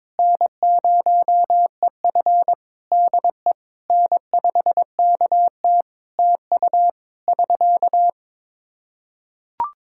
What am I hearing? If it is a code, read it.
Morse code, heard as a keyed tone, about 22 wpm, 700 Hz. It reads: N0EF DE N5KT TU <SK>
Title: Practice QSO - Exchange 4 - Ending - 22wpm